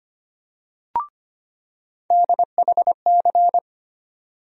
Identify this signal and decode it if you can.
Morse code, heard as a keyed tone, about 25 wpm, 700 Hz. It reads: DHC
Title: USA Call Sign Postfixes - 3 Letters 25wpm